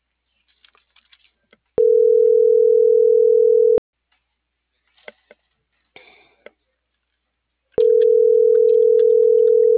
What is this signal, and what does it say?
Ringback tone (North America), 440 + 480 Hz.